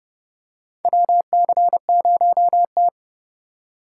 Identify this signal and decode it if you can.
Morse code, heard as a keyed tone, about 32 wpm, 700 Hz. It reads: WC0T
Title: USA Call Signs - 4 Characters - 3x Character Spacing - 30wpm